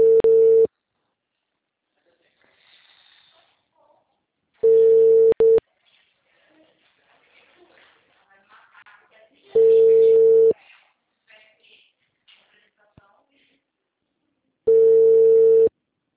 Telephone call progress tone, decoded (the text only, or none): ringback tone